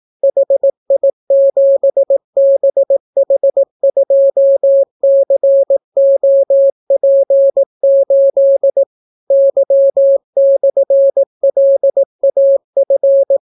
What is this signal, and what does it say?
Morse code, heard as a keyed tone, about 18 wpm, 550 Hz. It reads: HI7BH2COP8 Y/LAF